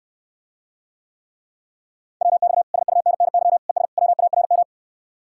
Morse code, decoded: WX SUNNY ES WARM

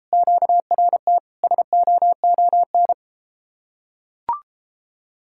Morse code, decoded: QRT SOON

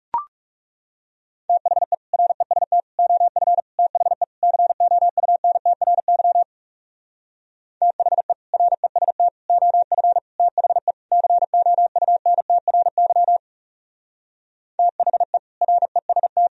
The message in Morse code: THE REST OF THE COUNTRY THE REST OF THE COUNTRY THE REST

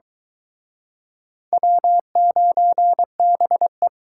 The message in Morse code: W9BE

